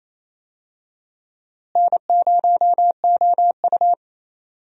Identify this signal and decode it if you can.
Morse code, heard as a keyed tone, about 28 wpm, 700 Hz. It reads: N0OU
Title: USA Call Signs - 4 Characters - Repeated 3x - 28wpm